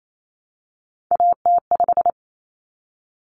Morse code, heard as a keyed tone, about 28 wpm, 700 Hz. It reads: AT5
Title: Common Cut Numbers - 3-Digit 28wpm